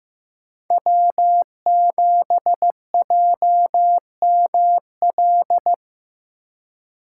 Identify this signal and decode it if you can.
Morse code, heard as a keyed tone, about 18 wpm, 700 Hz. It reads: W7JML
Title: Speed Racing - USA 1x3 Call Signs 15wpm